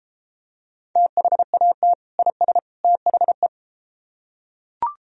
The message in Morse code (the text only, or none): THAT IS THE